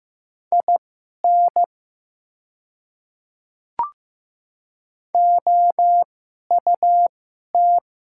Morse code, heard as a keyed tone, about 15 wpm, spaced as for 11 wpm, 700 Hz. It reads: IN OUT
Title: 2x Character Spacing - Top 100 Words - 15wpm